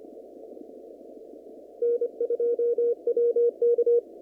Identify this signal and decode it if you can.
Morse code, heard as a keyed tone, about 29 wpm, 470 Hz. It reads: N2WK
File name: CW / kt5j wpx cw-op-k5pi